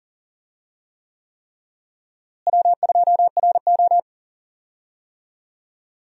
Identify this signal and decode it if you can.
Morse code, heard as a keyed tone, about 39 wpm, 700 Hz. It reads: W2RO
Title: USA Call Signs - 4 Characters - Repeated 3x - 40wpm